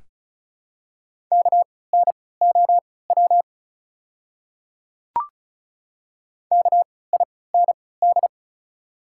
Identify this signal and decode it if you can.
Morse code, heard as a keyed tone, about 36 wpm, spaced as for 20 wpm, 700 Hz. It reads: KNOW KIND
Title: 3x Character Spacing - Top 200 Words - 35wpm